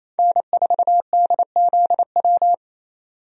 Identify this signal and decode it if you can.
Morse code, heard as a keyed tone, about 28 wpm, 700 Hz. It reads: N4DZW
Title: USA Call Signs - 5 Characters - 28wpm